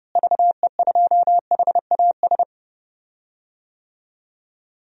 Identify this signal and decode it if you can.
Morse code, heard as a keyed tone, about 30 wpm, 700 Hz. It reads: VE2HAS
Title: DXCC-100 Call Signs - 6 Characters - Repeated 3x - 30wpm